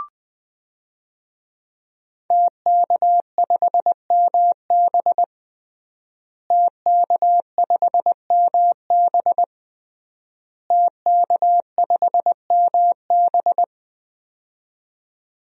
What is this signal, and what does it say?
Morse code, heard as a keyed tone, about 20 wpm, 700 Hz. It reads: TK5MB TK5MB TK5MB
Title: DXCC-100 Call Signs - 5 Characters - Repeated 3x - 20wpm